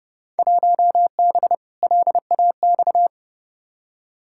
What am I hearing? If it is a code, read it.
Morse code, heard as a keyed tone, about 30 wpm, 700 Hz. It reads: 1B LAX
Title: Field Day Exchanges - 30wpm